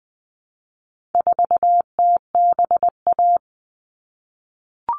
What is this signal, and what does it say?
Morse code, heard as a keyed tone, about 20 wpm, 700 Hz. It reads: 4TBA
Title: All Cut Numbers - 4-Digit 20wpm